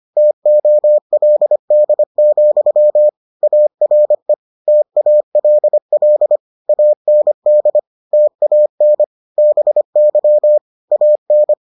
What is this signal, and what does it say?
Morse code, heard as a keyed tone, about 25 wpm, 600 Hz. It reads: TOLD, ARE TALL AND TAN BY AN